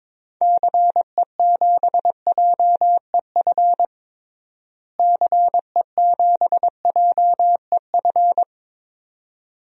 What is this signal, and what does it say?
Morse code, heard as a keyed tone, about 22 wpm, 700 Hz. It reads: CE7JEF CE7JEF
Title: DXCC-100 Call Signs - 6 Characters - Repeated 2x - 22wpm